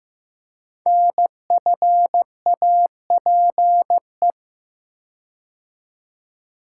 Morse code, heard as a keyed tone, about 15 wpm, 700 Hz. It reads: NFAPE
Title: Code Groups - 5 Letters - 15wpm